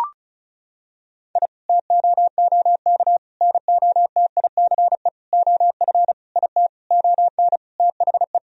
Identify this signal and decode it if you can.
Morse code, heard as a keyed tone, about 36 wpm, 700 Hz. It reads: I TOOK NOTICE OF IT ON THE
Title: Sentences from Top 1000 Words - Review Part 2 - 35wpm